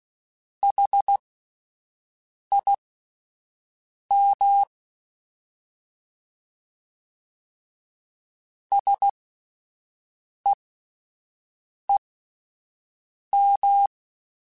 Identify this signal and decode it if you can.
Morse code, heard as a keyed tone, about 16 wpm, spaced as for 5 wpm, 800 Hz. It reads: HIM SEEM
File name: S vs H / S vs H mp3s / S vs H @5